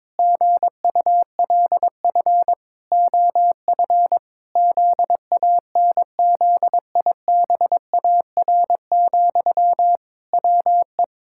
Morse code, heard as a keyed tone, about 22 wpm, 700 Hz. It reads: GULF OF ZANZIBAR, WE
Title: Pangram Challenge v2 - 22wpm